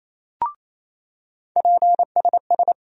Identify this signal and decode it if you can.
Morse code, heard as a keyed tone, about 28 wpm, 700 Hz. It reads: PSS
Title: USA Call Sign Postfixes - 3 Letters 28wpm